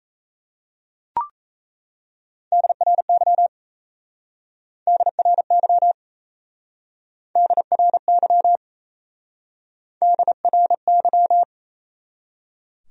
Morse code, heard as a keyed tone, about 35 wpm, 700 Hz. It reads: DRY DRY DRY DRY
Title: Speed-Racing Top 500 Words 28wpm